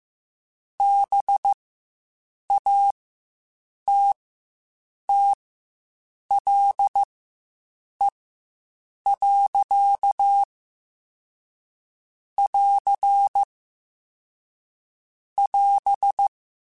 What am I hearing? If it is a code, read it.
Morse code, heard as a keyed tone, about 15 wpm, spaced as for 7 wpm, 800 Hz. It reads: BATTLE. <AR> <AS>
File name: Quotes Spaced / Quotes Spaced-mp3s / Pessimism sp @10